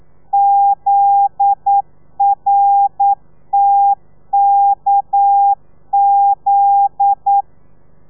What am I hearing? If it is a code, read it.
Morse code, heard as a keyed tone, about 9 wpm, 800 Hz. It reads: ZRTKZ
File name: telegrafia / i2tfj / Lez9 40